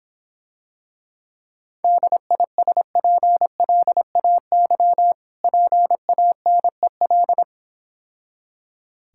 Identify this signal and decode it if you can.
Morse code, heard as a keyed tone, about 26 wpm, 700 Hz. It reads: DISPLAY PANEL